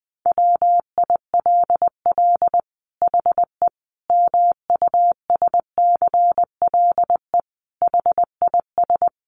Morse code, heard as a keyed tone, about 20 wpm, 700 Hz. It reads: WILL HE MUSCLE HIS
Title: Sentences from Top 1300 Words 20wpm